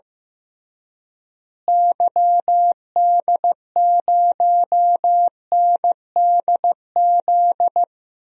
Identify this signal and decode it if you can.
Morse code, heard as a keyed tone, about 15 wpm, 700 Hz. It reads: YD0NDZ
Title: DXCC-100 Call Signs - 6 Characters - Repeated 3x - 15wpm